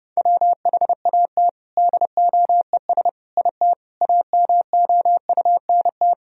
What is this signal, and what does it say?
Morse code, heard as a keyed tone, about 30 wpm, 700 Hz. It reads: WHAT DOES IT AMOUNT
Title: Sentences from Top 1000 Words - Review Part 4 - 30wpm